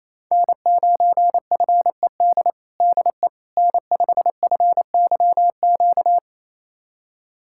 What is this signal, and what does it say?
Morse code, heard as a keyed tone, about 28 wpm, 700 Hz. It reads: N9FED DE N5FYQ
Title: USA Callsign DE Callsign - 5 Characters - 28wpm